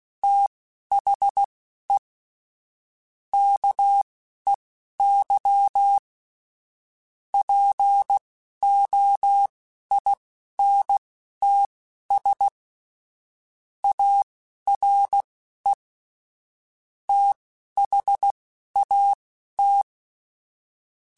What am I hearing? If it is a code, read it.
Morse code, heard as a keyed tone, about 16 wpm, spaced as for 11 wpm, 800 Hz. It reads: THE KEY POINTS ARE THAT